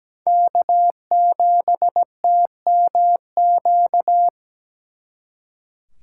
Morse code, heard as a keyed tone, about 20 wpm, 700 Hz. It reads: K7TMQ
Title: Speed Racing - USA 1x3 Call Signs 17wpm